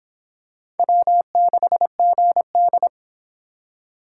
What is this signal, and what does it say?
Morse code, heard as a keyed tone, about 25 wpm, 700 Hz. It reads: W6GD